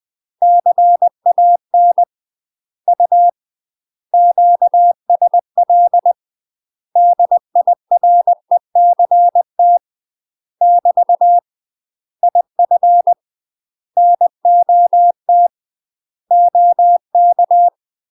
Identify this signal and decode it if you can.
Morse code, heard as a keyed tone, about 20 wpm, 700 Hz. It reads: CAN U QSL DIRECT <BT> IF NOT OK
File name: UK MASTER QSOs TRAINING / 20WPM / UK MASTER QSOs 0073